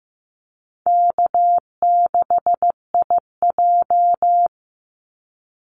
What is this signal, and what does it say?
Morse code, heard as a keyed tone, about 15 wpm, 700 Hz. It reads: K6IJ
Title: Easy Call Signs Repeated 2x - 15wpm